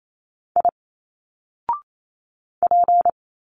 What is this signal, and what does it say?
Morse code, heard as a keyed tone, about 28 wpm, 700 Hz. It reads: I P
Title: Single Letters 28wpm